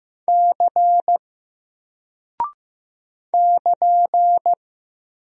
Morse code, heard as a keyed tone, about 15 wpm, 700 Hz. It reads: C <KN>
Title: Single Character (Letter:Number:Punctuation:Prosign) 15wpm